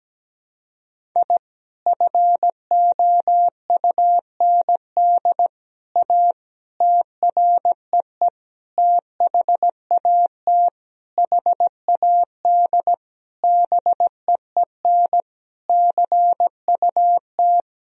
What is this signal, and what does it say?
Morse code, heard as a keyed tone, about 17 wpm, 700 Hz. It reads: I FOUND A TREE THAT HAD BEEN CUT
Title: Sentences from Top 1000 Words - Review Part 1 - 17wpm